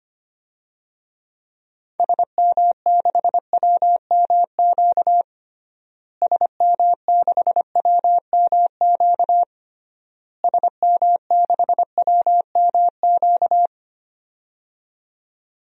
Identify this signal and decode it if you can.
Morse code, heard as a keyed tone, about 25 wpm, 700 Hz. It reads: SM6WMQ SM6WMQ SM6WMQ
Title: DXCC-100 Call Signs - 6 Characters - Repeated 3x - 25wpm